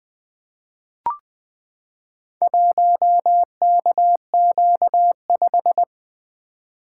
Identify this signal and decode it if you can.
Morse code, heard as a keyed tone, about 20 wpm, 700 Hz. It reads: 1KQ5
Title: Four Letter-Number Combinations - 20wpm